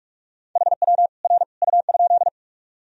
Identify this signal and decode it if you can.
Morse code, heard as a keyed tone, about 46 wpm, 700 Hz. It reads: HW R U?
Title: Sentences from Top 100 Words and Abbreviations 45wpm